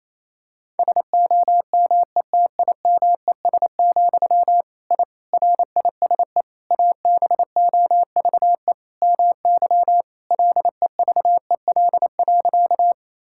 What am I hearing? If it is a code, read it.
Morse code, heard as a keyed tone, about 28 wpm, 700 Hz. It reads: SOMETIMES, I RISE ABOVE MY LEVEL.
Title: Sentences from Top 1000 Words - Review Part 6 - 28wpm